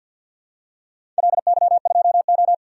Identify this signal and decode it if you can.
Morse code, heard as a keyed tone, about 51 wpm, 700 Hz. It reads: LY2O
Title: DXCC-100 Call Signs - 4 Characters - 50wpm